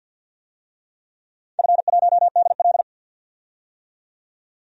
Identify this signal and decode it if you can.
Morse code, heard as a keyed tone, about 52 wpm, 700 Hz. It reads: F1DL
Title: DXCC-100 Call Signs - 4 Characters - 50wpm